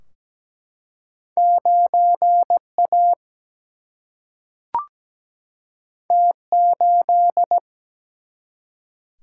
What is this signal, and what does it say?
Morse code, heard as a keyed tone, about 17 wpm, 700 Hz. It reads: 9A T8
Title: Single-Letter Single-Number 17wpm